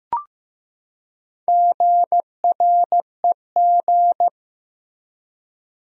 Morse code, heard as a keyed tone, about 15 wpm, 700 Hz. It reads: GREG
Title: USA Common Names - Max 5 Letters - 15wpm